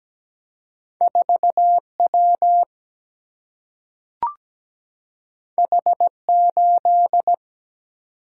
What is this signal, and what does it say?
Morse code, heard as a keyed tone, about 17 wpm, 700 Hz. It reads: 4 W H 8
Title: Single-Letter Single-Number 17wpm